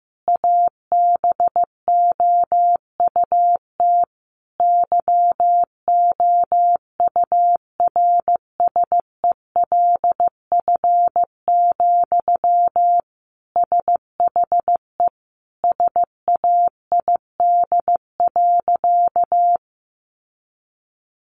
Morse code, heard as a keyed tone, about 15 wpm, 700 Hz. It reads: ABOUT YOURSELF, SHE SAID.
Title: Sentences from Top 600 Words 15wpm